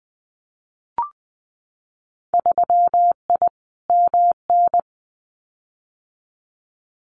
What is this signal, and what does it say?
Morse code, heard as a keyed tone, about 20 wpm, 700 Hz. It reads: 3I MN
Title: Winter Field Day Exchanges 20wpm